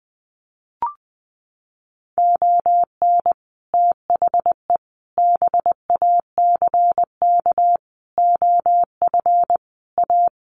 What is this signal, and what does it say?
Morse code, heard as a keyed tone, about 20 wpm, 700 Hz. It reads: ON THE BACK OF A
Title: Sets of 5 Words Encore2 20wpm